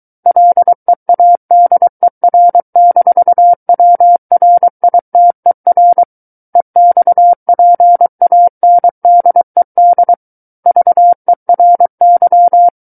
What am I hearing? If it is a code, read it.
Morse code, heard as a keyed tone, about 23 wpm, 700 Hz. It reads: LEADER-WRITER EXPANDED VERY
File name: Book One / 14-Chapter 14 - 23 WPM